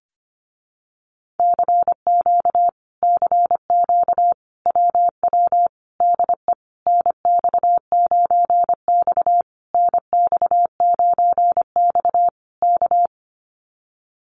Morse code, heard as a keyed tone, about 25 wpm, 700 Hz. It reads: CQ CQ WW DE NX9X NX9X K
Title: Common QSO Phrases Encore 25wpm